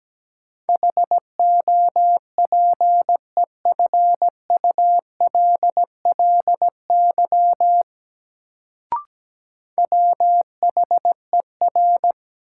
Morse code, heard as a keyed tone, about 17 wpm, 700 Hz. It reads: HOPEFULLY WHER
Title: Sentences from Top 1500 Words - New Word Drill Alt - 17wpm